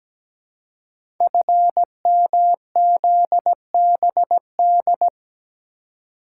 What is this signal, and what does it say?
Morse code, heard as a keyed tone, about 17 wpm, 700 Hz. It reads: FMZBD
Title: Code Groups - 5 Letters - 17wpm